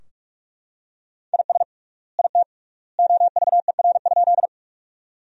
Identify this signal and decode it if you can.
Morse code, heard as a keyed tone, about 41 wpm, 690 Hz. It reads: IS IT OVER?
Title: 3x Word-Spacing - Sentences from Top 100 Words - 45wpm